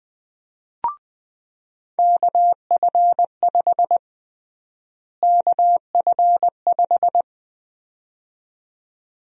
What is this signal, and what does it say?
Morse code, heard as a keyed tone, about 20 wpm, 700 Hz. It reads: KF5 KF5
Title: USA Call Sign Prefixes - Repeated 2x - 20wpm